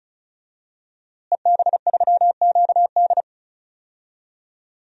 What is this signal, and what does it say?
Morse code, heard as a keyed tone, about 36 wpm, 700 Hz. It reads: EB3QD
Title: DXCC-100 Call Signs - 5 Characters - Repeated 2x - 35wpm